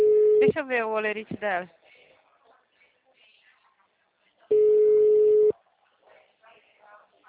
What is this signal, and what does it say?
Ringback tone (Europe, 425 Hz), with speech.